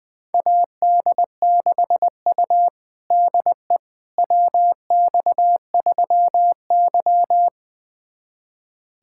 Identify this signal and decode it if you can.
Morse code, heard as a keyed tone, about 20 wpm, 700 Hz. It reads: AD6U DE WX3Y
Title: USA Callsign DE Callsign - 4 Characters - 20wpm